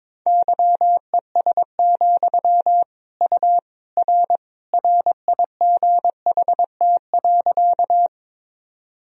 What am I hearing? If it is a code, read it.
Morse code, heard as a keyed tone, about 22 wpm, 700 Hz. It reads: YES, U R RIGHT.
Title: Sentences from Top 500 Words and Abbreviations 22wpm